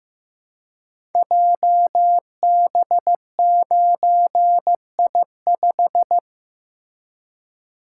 Morse code, heard as a keyed tone, about 15 wpm, 700 Hz. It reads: JB9I5
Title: Code Groups - 5 Letter:Numbers - 15wpm